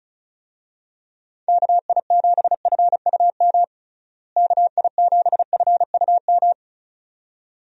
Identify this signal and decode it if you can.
Morse code, heard as a keyed tone, about 36 wpm, 700 Hz. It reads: KI7FUM KI7FUM
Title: USA Call Signs - 6 Characters - Repeated 2x - 35wpm